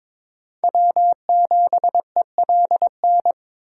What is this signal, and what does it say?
Morse code, heard as a keyed tone, about 22 wpm, 700 Hz. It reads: W7ELN
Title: USA Call Signs - 5 Characters - 4x Character Spacing - 22wpm